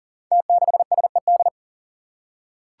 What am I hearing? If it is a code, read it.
Morse code, heard as a keyed tone, about 42 wpm, 700 Hz. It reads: T6SED
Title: Five Challenging Letter-Number Combinations 40wpm